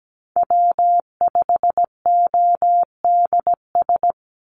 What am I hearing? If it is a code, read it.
Morse code, heard as a keyed tone, about 17 wpm, 700 Hz. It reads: W5ODS
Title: NA SOTA Chaser Callsigns 17wpm